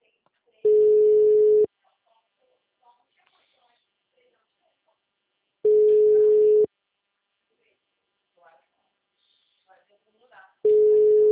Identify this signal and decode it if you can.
Ringback tone (Europe, 425 Hz), with speech.